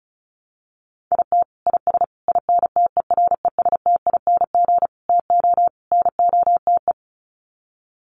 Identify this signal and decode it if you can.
Morse code, heard as a keyed tone, about 34 wpm, 700 Hz. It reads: IT IS INTERESTING TO NOTE
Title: Sets of 5 Words Encore2 35wpm